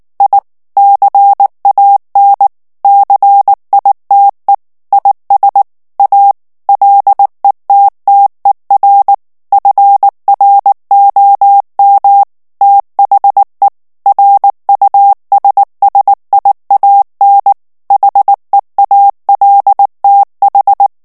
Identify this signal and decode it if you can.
Morse code, heard as a keyed tone, about 19 wpm, 800 Hz. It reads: I CAN CITE IS A LETTER FROM THE RUSSIAN HEALTH
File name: morse mp3 / ANGLAIS texte clair 2 vitesse 18 mots minute